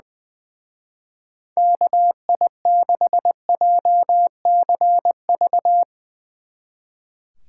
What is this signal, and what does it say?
Morse code, heard as a keyed tone, about 22 wpm, 700 Hz. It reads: KI6JCV
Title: Speed Racing - USA 2x3 Call Signs 20wpm